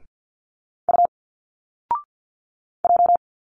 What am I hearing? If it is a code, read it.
Morse code, heard as a keyed tone, about 50 wpm, 700 Hz. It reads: U J
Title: Single Letters 50wpm